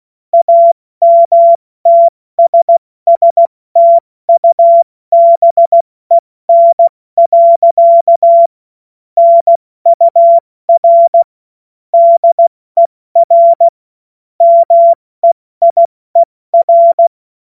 Morse code, heard as a keyed tone, about 16 wpm, spaced as for 14 wpm, 680 Hz. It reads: AMTSSTUBEN. NUR DER MEIER
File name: Witz-07-Oktober-2025-70-F80